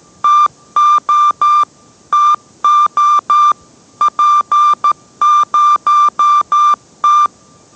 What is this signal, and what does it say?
Morse code, heard as a keyed tone, about 13 wpm, 1190 Hz. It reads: TO TO P0T